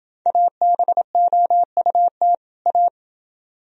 Morse code, heard as a keyed tone, about 30 wpm, 700 Hz. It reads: ABOUT A